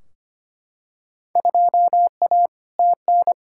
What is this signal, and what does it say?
Morse code, heard as a keyed tone, about 25 wpm, 700 Hz. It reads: 2A TN
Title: Field Day Exchanges - 25wpm